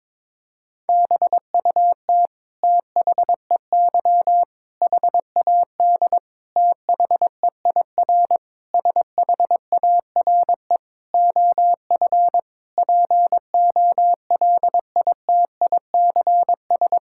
Morse code, heard as a keyed tone, about 22 wpm, 700 Hz. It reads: BUT THEY HAD THEIR SHARE OF POLITICS
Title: Sentences from Top 1000 Words - Review Part 6 - 22wpm